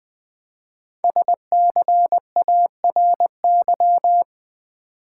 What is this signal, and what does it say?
Morse code, heard as a keyed tone, about 20 wpm, 700 Hz. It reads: SCARY
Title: 5 Letter Words - First 3 Letters Given - 20wpm